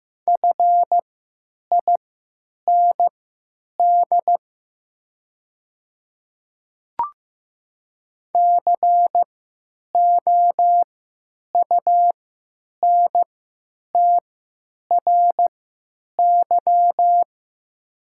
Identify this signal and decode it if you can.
Morse code, heard as a keyed tone, about 15 wpm, spaced as for 9 wpm, 700 Hz. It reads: FIND COUNTRY
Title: 3x Character Spacing - Top 300 Words - 15wpm